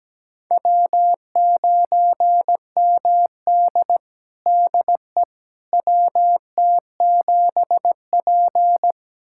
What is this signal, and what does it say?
Morse code, heard as a keyed tone, about 17 wpm, 700 Hz. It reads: W9MD DE WT7P
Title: USA Callsign DE Callsign - 4 Characters - 17wpm